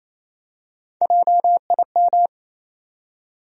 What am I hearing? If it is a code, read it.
Morse code, heard as a keyed tone, about 28 wpm, 700 Hz. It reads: JIM
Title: USA Common Names - Max 3 Letters - 28wpm